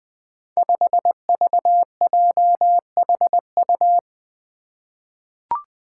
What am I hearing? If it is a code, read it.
Morse code, heard as a keyed tone, about 20 wpm, 700 Hz. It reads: 5VJHU
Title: Five Challenging Letter-Number Combinations 20wpm